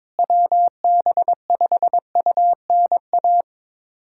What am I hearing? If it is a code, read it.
Morse code, heard as a keyed tone, about 22 wpm, 700 Hz. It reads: WB5UNA
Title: USA Callsign DE Callsign - 6 Characters - 22wpm